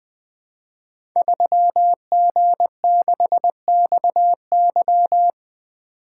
Morse code, heard as a keyed tone, about 20 wpm, 700 Hz. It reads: 3G6XY
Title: Code Groups - 5 Letter:Numbers - 20wpm